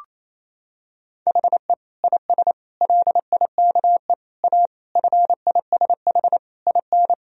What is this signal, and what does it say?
Morse code, heard as a keyed tone, about 28 wpm, 700 Hz. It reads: HE IS LIKE A FISH IN
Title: Sentences from Top 1000 Words - Review Part 1 - 28wpm